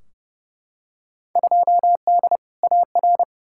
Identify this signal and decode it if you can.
Morse code, heard as a keyed tone, about 29 wpm, 700 Hz. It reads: 2D AR
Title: Field Day Exchanges Encore - 30wpm